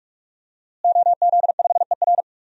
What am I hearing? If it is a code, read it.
Morse code, heard as a keyed tone, about 42 wpm, 700 Hz. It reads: OZ5ER
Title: DXCC-100 Call Signs - 5 Characters - Repeated 3x - 45wpm